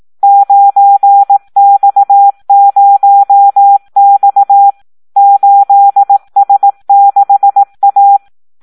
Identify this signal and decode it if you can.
Morse code, heard as a keyed tone, about 18 wpm, 800 Hz. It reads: 9X0X 8S6A
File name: Audio practica CW / SD 1 18